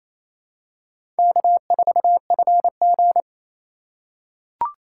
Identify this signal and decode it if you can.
Morse code, heard as a keyed tone, about 27 wpm, 700 Hz. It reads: K4FG
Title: USA Call Signs - 4 Characters - 4x Character Spacing - 28wpm